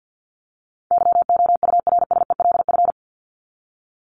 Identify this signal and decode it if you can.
Morse code, heard as a keyed tone, about 50 wpm, 700 Hz. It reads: YOURSELF
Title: Top 501-750 Words 50wpm